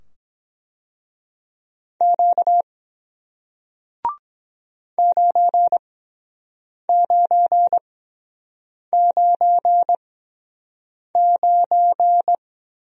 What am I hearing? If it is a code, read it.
Morse code, heard as a keyed tone, about 23 wpm, spaced as for 7 wpm, 700 Hz. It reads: Q 9999